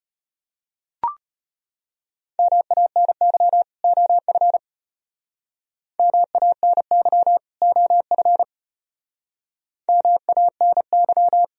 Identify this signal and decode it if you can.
Morse code, heard as a keyed tone, about 32 wpm, 700 Hz. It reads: MANY OF MANY OF MANY